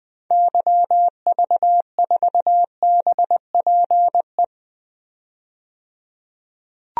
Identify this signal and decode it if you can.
Morse code, heard as a keyed tone, about 21 wpm, 700 Hz. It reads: YV4BPE
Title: DXCC-100 Call Signs - 6 Characters - 20wpm